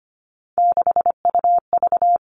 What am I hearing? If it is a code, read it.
Morse code, heard as a keyed tone, about 25 wpm, 700 Hz. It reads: 6UV
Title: All Cut Numbers - 3-Digit 25wpm